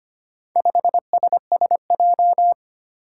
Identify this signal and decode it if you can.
Morse code, heard as a keyed tone, about 25 wpm, 700 Hz. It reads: 5SSJ
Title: Four Challenging Letter-Number Combinations 25wpm